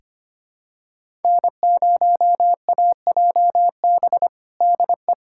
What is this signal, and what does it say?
Morse code, heard as a keyed tone, about 25 wpm, 700 Hz. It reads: N0AJB DE
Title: USA Callsign DE Callsign - 5 Characters - 25wpm